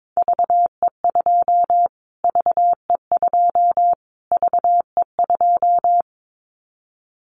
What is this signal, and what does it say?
Morse code, heard as a keyed tone, about 22 wpm, 700 Hz. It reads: VE2 VE2 VE2
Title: Callsign Prefixes for Top 25 DXCC - Repeated 3x - 22wpm